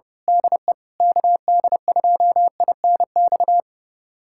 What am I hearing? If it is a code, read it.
Morse code, heard as a keyed tone, about 32 wpm, 700 Hz. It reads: DE KD2INX
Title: USA Callsign DE Callsign 30wpm